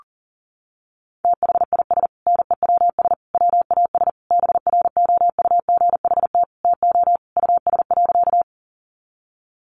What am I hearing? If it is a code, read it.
Morse code, heard as a keyed tone, about 40 wpm, 700 Hz. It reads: THIS NEWS WAS BROUGHT TO US.
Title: Sentences from Top 800 Words 40wpm